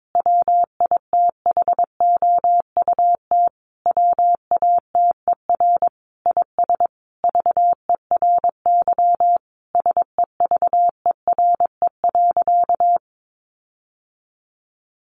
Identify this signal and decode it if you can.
Morse code, heard as a keyed tone, about 22 wpm, 700 Hz. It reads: WITHOUT WATER IS VERY SEVERE.
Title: Sentences from Top 1500 Words 22wpm